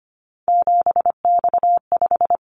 Morse code, heard as a keyed tone, about 25 wpm, 700 Hz. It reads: 7X5
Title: Callsign Prefixes for Top 100 DXCC - 25wpm